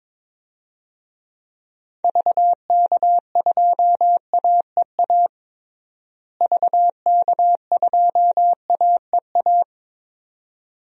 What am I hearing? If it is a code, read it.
Morse code, heard as a keyed tone, about 22 wpm, 700 Hz. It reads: VK2AEA VK2AEA
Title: DXCC-100 Call Signs - 6 Characters - Repeated 2x - 22wpm